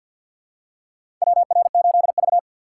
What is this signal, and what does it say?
Morse code, heard as a keyed tone, about 54 wpm, 700 Hz. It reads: WR8V
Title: USA Call Signs - 4 Characters - 6x Character Spacing - 50wpm